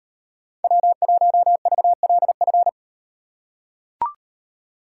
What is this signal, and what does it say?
Morse code, heard as a keyed tone, about 35 wpm, 700 Hz. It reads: W1VLF